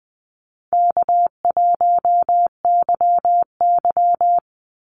Morse code, heard as a keyed tone, about 20 wpm, 700 Hz. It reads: K1YY
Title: Easy Call Signs Repeated 2x - 20wpm